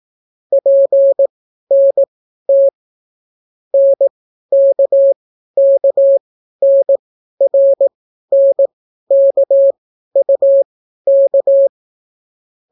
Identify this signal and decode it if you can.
Morse code, heard as a keyed tone, about 18 wpm, spaced as for 12 wpm, 550 Hz. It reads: PNT NKKNRNKUK